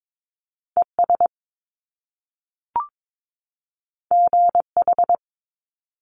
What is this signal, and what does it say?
Morse code, heard as a keyed tone, about 22 wpm, 700 Hz. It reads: ES GH
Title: Top Two Letter Combinations in Words 22wpm